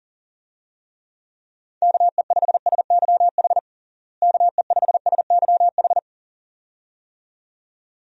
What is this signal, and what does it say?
Morse code, heard as a keyed tone, about 40 wpm, 700 Hz. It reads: KE5SYH KE5SYH
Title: USA Call Signs - 6 Characters - Repeated 2x - 40wpm